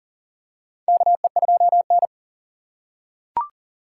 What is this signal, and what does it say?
Morse code, heard as a keyed tone, about 40 wpm, 700 Hz. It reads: KE2N
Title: USA Call Signs - 4 Characters - 4x Character Spacing - 40wpm